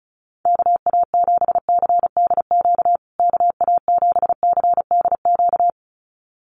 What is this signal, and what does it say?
Morse code, heard as a keyed tone, about 34 wpm, 700 Hz. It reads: KA7CDQ KA7CDQ
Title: USA Call Signs Repeated 2x 35wpm